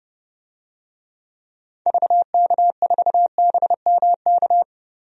Morse code, heard as a keyed tone, about 30 wpm, 700 Hz. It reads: VK4BMK
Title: DXCC-100 Call Signs - 6 Characters - 30wpm